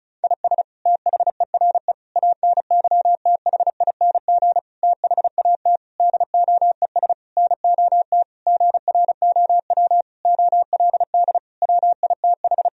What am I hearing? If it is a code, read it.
Morse code, heard as a keyed tone, about 36 wpm, 700 Hz. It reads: IS THERE ANYTHING THAT DOES NOT GROW OLD WITH
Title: Sentences from Top 1000 Words - Review Part 1 - 35wpm